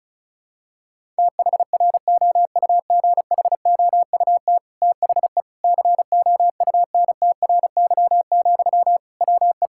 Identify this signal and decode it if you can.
Morse code, heard as a keyed tone, about 34 wpm, 700 Hz. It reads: THROUGHOUT THE COUNTRY, WE
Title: Sentences from Top 1000 Words - Review Part 5 - 35wpm